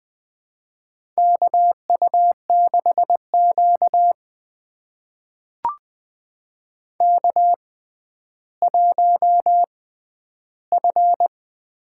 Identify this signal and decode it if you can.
Morse code, heard as a keyed tone, about 20 wpm, spaced as for 11 wpm, 700 Hz. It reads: KU6Q K1F
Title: USA Call Signs - 4 Characters - 6x Character Spacing - 20wpm